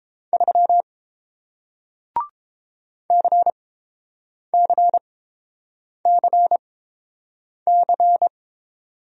Morse code, heard as a keyed tone, about 30 wpm, spaced as for 7 wpm, 700 Hz. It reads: 3 CCCC